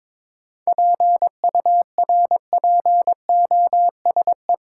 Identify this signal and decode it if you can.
Morse code, heard as a keyed tone, about 22 wpm, 700 Hz. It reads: PURPOSE
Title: Sentences from Top 800 Words - New Word Drill - 22wpm